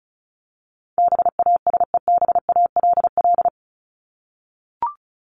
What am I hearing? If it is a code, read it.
Morse code, heard as a keyed tone, about 34 wpm, 700 Hz. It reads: BASEBALL
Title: Top 1751-2000 Words 35wpm